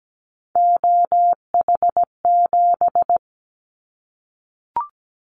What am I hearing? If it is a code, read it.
Morse code, heard as a keyed tone, about 17 wpm, 700 Hz. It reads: OH7
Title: Callsign Prefixes for Top 50 DXCC - 17wpm